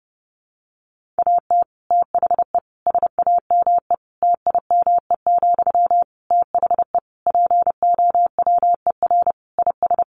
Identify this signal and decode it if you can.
Morse code, heard as a keyed tone, about 30 wpm, 700 Hz. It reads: AT THE SAME TIME, THE POWER IS
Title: Sentences from Top 1500 Words 30wpm